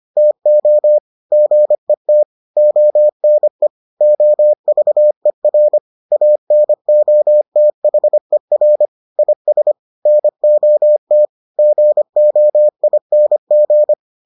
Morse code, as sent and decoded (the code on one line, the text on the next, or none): - --- / --. - / --- -. / --- ...- . .-. / .- -. --- - .... .-. / .. ... / -. --- - / --. --- .. -. --.
TO GET ONE OVER ANOTHER IS NOT GOING